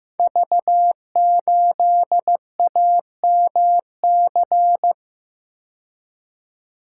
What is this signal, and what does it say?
Morse code, heard as a keyed tone, about 15 wpm, 700 Hz. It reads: V8AMC
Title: Code Groups - 5 Letter:Numbers - 15wpm